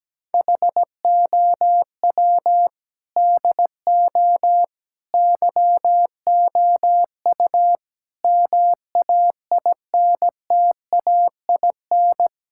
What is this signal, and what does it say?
Morse code, heard as a keyed tone, about 17 wpm, 700 Hz. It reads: HOW DO YOU MAINTAIN
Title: Sentences from Top 1000 Words - Review Part 6 - 17wpm